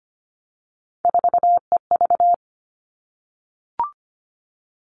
Morse code, heard as a keyed tone, about 25 wpm, 700 Hz. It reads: 4EV
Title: All Cut Numbers - 3-Digit 25wpm